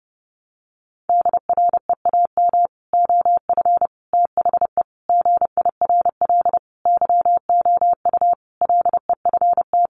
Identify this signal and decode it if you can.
Morse code, heard as a keyed tone, about 28 wpm, 700 Hz. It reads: DREAM OF THE GIRL YOU LEFT
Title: Sentences from Top 1500 Words 30wpm